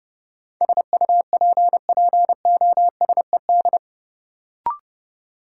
Morse code, heard as a keyed tone, about 31 wpm, 700 Hz. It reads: SUPPOSED
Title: Sets of 3 Words - New Word Drill - 30wpm